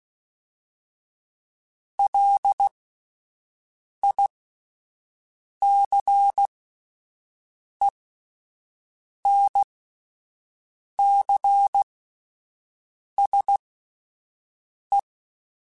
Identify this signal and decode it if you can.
Morse code, heard as a keyed tone, about 16 wpm, spaced as for 5 wpm, 800 Hz. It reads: LICENCSE
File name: QSOs Spaced